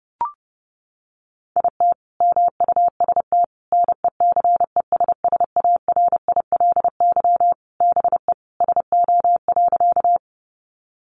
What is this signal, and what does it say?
Morse code, heard as a keyed tone, about 30 wpm, 700 Hz. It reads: IT MUST NECESSARILY BE SO.
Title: Sentences from Top 1400 Words 30wpm